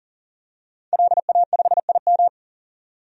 Morse code, heard as a keyed tone, about 40 wpm, 700 Hz. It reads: LA5IM
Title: DXCC-100 Call Signs - 5 Characters - 40wpm